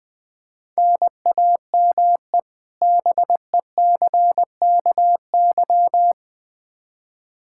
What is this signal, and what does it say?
Morse code, heard as a keyed tone, about 21 wpm, 700 Hz. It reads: NAME BECKY
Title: K1USN SST QSO Practice 20wpm